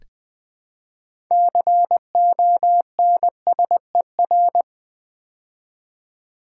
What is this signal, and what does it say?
Morse code, heard as a keyed tone, about 20 wpm, 700 Hz. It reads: CONSER
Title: Build A Word - 3 Letters At A Time - 8-Letter Words - 20wpm